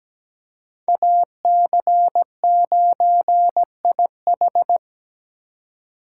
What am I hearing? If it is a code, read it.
Morse code, heard as a keyed tone, about 17 wpm, 700 Hz. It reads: AC9IH
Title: USA Call Signs - 4x Character Spacing - 17wpm